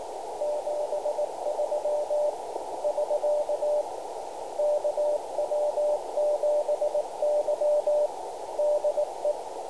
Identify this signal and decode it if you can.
Morse code, heard as a keyed tone, about 18 wpm, 600 Hz, with noise. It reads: 73 <SK> KW7Y DE
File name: camano / g0rek 80m cw